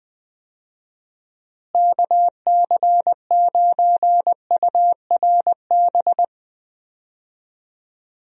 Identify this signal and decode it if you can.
Morse code, heard as a keyed tone, about 21 wpm, 700 Hz. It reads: KC9URB
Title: USA Call Signs - 6 Characters - Repeated 2x - 20wpm